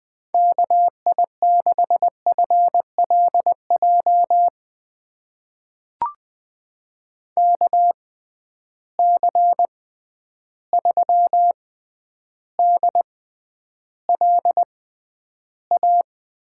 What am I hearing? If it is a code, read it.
Morse code, heard as a keyed tone, about 20 wpm, spaced as for 10 wpm, 700 Hz. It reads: KI6FLJ KC3DLA
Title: USA Call Signs - 6 Characters - 6x Character Spacing - 20wpm